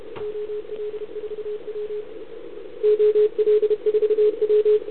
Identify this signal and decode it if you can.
Morse code, heard as a keyed tone, about 33 wpm, 440 Hz. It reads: OL4W OL4W
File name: CQ160 2022 CW / SP